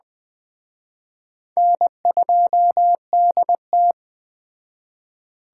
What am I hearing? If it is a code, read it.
Morse code, heard as a keyed tone, about 21 wpm, 700 Hz. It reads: N2DT